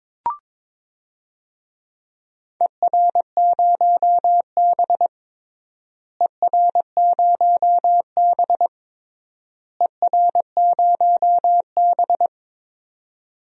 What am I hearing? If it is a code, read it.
Morse code, heard as a keyed tone, about 22 wpm, 700 Hz. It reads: ER0B ER0B ER0B
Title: DXCC-100 Call Signs - 4 Characters - Repeated 3x - 22wpm